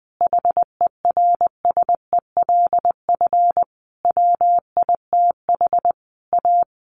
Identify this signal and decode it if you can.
Morse code, heard as a keyed tone, about 19 wpm, 700 Hz. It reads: HERSELF WITH A
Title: Easy Phrases 20wpm